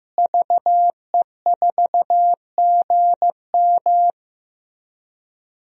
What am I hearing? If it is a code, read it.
Morse code, heard as a keyed tone, about 15 wpm, 700 Hz. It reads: VE4GM
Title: DXCC-100 Call Signs - 5 Characters - Repeated 3x - 15wpm